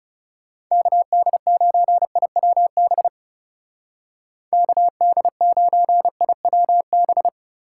Morse code, heard as a keyed tone, about 37 wpm, 700 Hz. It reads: KD9IWB KD9IWB
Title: Speed Racing - USA Call Signs 30wpm